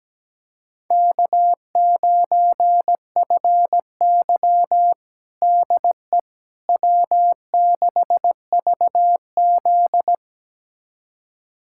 Morse code, heard as a keyed tone, about 17 wpm, 700 Hz. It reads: K9FY DE W6VZ
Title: USA Callsign DE Callsign - 4 Characters - 17wpm